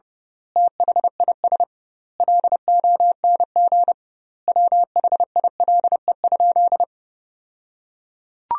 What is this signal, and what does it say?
Morse code, heard as a keyed tone, about 30 wpm, 700 Hz. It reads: THIS LONG WHILE?
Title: 2x Word-Spacing - Sentences from Top 300 Words - 30wpm